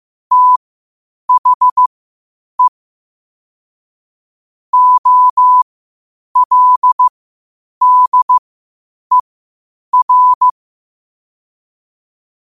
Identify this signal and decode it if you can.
Morse code, heard as a keyed tone, about 15 wpm, spaced as for 8 wpm, 1000 Hz. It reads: THE OLDER